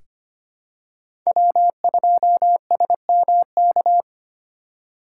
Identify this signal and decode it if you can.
Morse code, heard as a keyed tone, about 25 wpm, 700 Hz. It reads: W2SMK
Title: USA Call Signs - 5 Characters - 2x Character Spacing - 25wpm